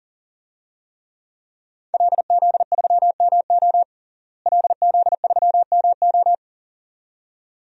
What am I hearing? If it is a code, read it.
Morse code, heard as a keyed tone, about 40 wpm, 700 Hz. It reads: LZ3MO LZ3MO
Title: DXCC-100 Call Signs - 5 Characters - Repeated 2x - 40wpm